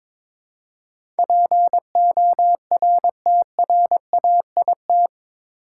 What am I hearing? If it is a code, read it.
Morse code, heard as a keyed tone, about 22 wpm, 700 Hz. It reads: PORTRAIT
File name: practice / 22wpm / Eight Letter Words - 22wpm